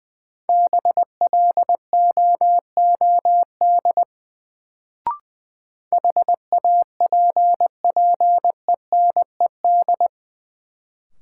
Morse code, heard as a keyed tone, about 20 wpm, 700 Hz. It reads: BLOOD HAPPENED
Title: Sentences from Top 1000 Words - Review Part 2 - New Word Drill - 20wpm